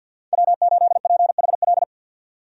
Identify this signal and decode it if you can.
Morse code, heard as a keyed tone, about 53 wpm, 700 Hz. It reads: W8PHL
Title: USA Call Signs - 5 Characters - 6x Character Spacing - 50wpm